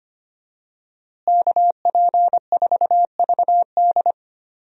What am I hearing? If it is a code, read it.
Morse code, heard as a keyed tone, about 25 wpm, 700 Hz. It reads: KP4VD
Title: DXCC-100 Call Signs - 5 Characters - Repeated 3x - 25wpm